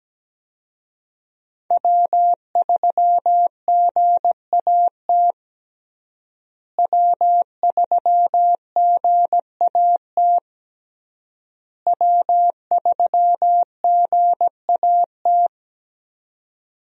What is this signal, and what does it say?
Morse code, heard as a keyed tone, about 17 wpm, 700 Hz. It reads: W3GAT W3GAT W3GAT
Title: USA Call Signs - 5 Characters - Repeated 3x - 17wpm